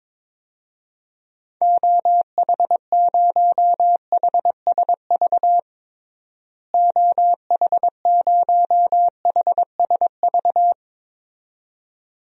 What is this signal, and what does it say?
Morse code, heard as a keyed tone, about 22 wpm, 700 Hz. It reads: OH0HSV OH0HSV
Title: DXCC-100 Call Signs - 6 Characters - Repeated 2x - 22wpm